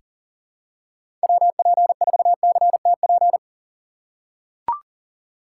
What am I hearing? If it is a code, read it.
Morse code, heard as a keyed tone, about 38 wpm, 700 Hz. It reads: WP4CTP
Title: USA Call Signs - 6x Character Spacing - 40wpm